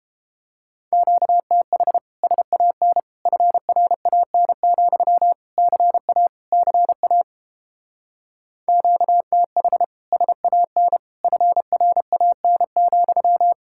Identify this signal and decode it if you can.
Morse code, heard as a keyed tone, about 31 wpm, 700 Hz. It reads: QTH SAN FRAN, CA CA QTH SAN FRAN,